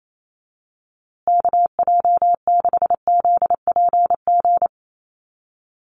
Morse code, heard as a keyed tone, about 28 wpm, 700 Hz. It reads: KJ6ZPG
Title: USA Call Signs Repeated 3x 28wpm